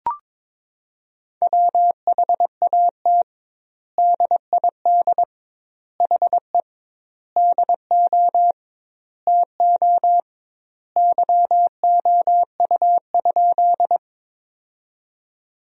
Morse code, heard as a keyed tone, about 22 wpm, 700 Hz. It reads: WHAT DID HE DO TO YOU?
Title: 2x Word-Spacing - Sentences from Top 100 Words - 22wpm